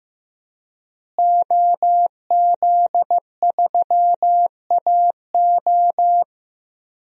Morse code, heard as a keyed tone, about 15 wpm, 700 Hz. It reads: OZ3AO
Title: DXCC-100 Call Signs - 5 Characters - Repeated 2x - 15wpm